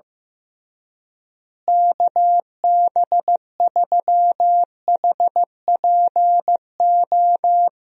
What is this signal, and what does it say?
Morse code, heard as a keyed tone, about 15 wpm, 700 Hz. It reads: KB3HPO
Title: USA Call Signs - 6 Characters - Repeated 3x - 15wpm